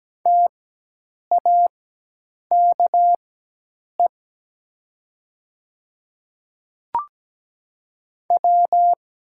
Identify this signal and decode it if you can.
Morse code, heard as a keyed tone, about 17 wpm, spaced as for 8 wpm, 700 Hz. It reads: TAKE W
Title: 4x Character Spacing - Top 200 Words - 17wpm